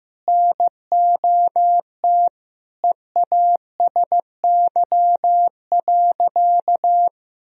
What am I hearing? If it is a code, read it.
Morse code, heard as a keyed tone, about 15 wpm, 700 Hz. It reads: NOT EASY.
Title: Sentences from Top 1000 Words - Review Part 2 - 15wpm